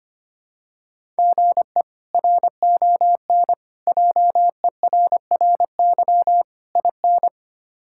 Morse code, heard as a keyed tone, about 25 wpm, 700 Hz. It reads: GE RON JERRY IN
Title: K1USN SST QSO Practice - 25wpm